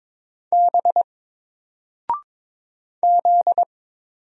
Morse code, heard as a keyed tone, about 22 wpm, 700 Hz. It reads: B Z
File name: Single-Letter-Number-22wpm